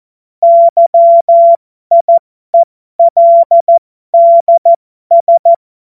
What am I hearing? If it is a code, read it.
Morse code, heard as a keyed tone, about 14 wpm, spaced as for 12 wpm, 680 Hz. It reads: YIELDS
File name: Morsekurs / Witze